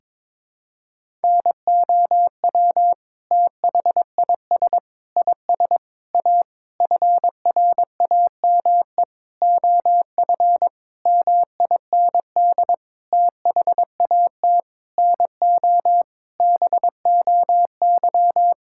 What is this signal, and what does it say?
Morse code, heard as a keyed tone, about 22 wpm, 700 Hz. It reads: NOW THIS IS A FRAME OF MIND THAT NO BOY
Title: Sentences from Top 1000 Words - Review Part 6 - 22wpm